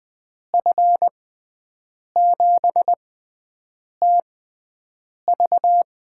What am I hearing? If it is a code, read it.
Morse code, heard as a keyed tone, about 20 wpm, spaced as for 7 wpm, 700 Hz. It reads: F7TV
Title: USA Call Signs - 6 Characters - 6x Character Spacing - 20wpm